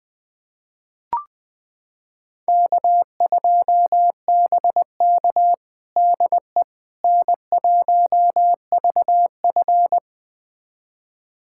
Morse code, heard as a keyed tone, about 20 wpm, 700 Hz. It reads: K2BK DE N1VF
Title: USA Callsign DE Callsign - 4 Characters - 20wpm